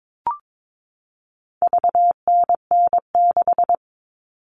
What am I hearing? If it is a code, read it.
Morse code, heard as a keyed tone, about 22 wpm, 700 Hz. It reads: VNN6
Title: All Cut Numbers - 4-Digit 22wpm